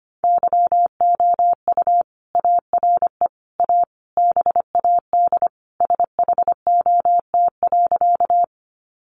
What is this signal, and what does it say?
Morse code, heard as a keyed tone, about 25 wpm, 700 Hz. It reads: YOU ARE A BAD SHOT.
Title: Sentences from Top 1100 Words 25wpm